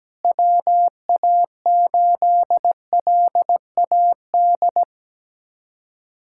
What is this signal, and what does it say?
Morse code, heard as a keyed tone, about 17 wpm, 700 Hz. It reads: WA8LAD
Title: USA Callsign DE Callsign 17wpm